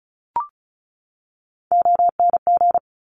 Morse code, heard as a keyed tone, about 36 wpm, 700 Hz. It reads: ONG
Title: Top Three Letter Combinations in Words 35wpm